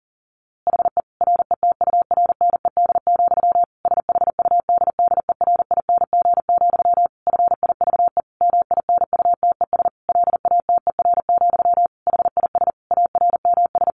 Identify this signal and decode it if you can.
Morse code, heard as a keyed tone, about 40 wpm, 700 Hz. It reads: HE RETURNED, SHUDDERING, FIVE MINUTES LATER, HIS ARMS
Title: A Sound Of Thunder 40wpm